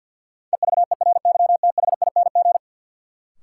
Morse code, heard as a keyed tone, about 47 wpm, 700 Hz. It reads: EVERYTHING
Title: Sentences from Top 700 Words - New Word Drill - 50wpm